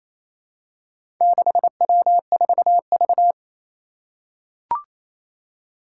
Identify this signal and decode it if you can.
Morse code, heard as a keyed tone, about 28 wpm, 700 Hz. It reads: 6W4V
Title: Four Challenging Letter-Number Combinations 28wpm